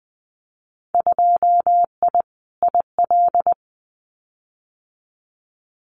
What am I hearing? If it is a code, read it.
Morse code, heard as a keyed tone, about 19 wpm, 700 Hz. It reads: 2I IL
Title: Winter Field Day Exchanges 20wpm